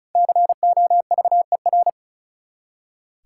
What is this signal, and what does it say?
Morse code, heard as a keyed tone, about 36 wpm, 700 Hz. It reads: COVER
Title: Five Letter Words - 35wpm